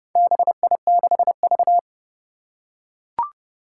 Morse code, heard as a keyed tone, about 30 wpm, 700 Hz. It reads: BI6V
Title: Four Challenging Letter-Number Combinations 30wpm